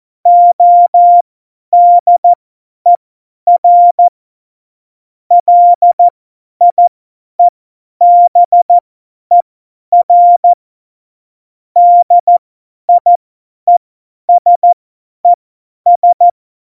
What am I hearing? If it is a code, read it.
Morse code, heard as a keyed tone, about 14 wpm, spaced as for 10 wpm, 700 Hz. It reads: ODER LIEBER DIESES
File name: Morsekurs / Witze / Witz 22. Januar 2019, 50 F70